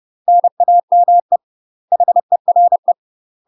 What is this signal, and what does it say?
Morse code, heard as a keyed tone, about 30 wpm, 700 Hz. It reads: NAME HERE
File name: UK MASTER QSOs TRAINING / 30WPM / UK MASTER QSOs 0021